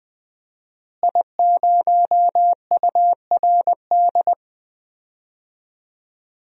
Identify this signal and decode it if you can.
Morse code, heard as a keyed tone, about 20 wpm, 700 Hz. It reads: I0URD
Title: DXCC-100 Call Signs - 5 Characters - Repeated 3x - 20wpm